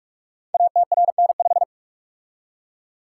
Morse code, heard as a keyed tone, about 44 wpm, 700 Hz. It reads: ATRN5
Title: Code Groups - 5 Letter:Numbers - 45wpm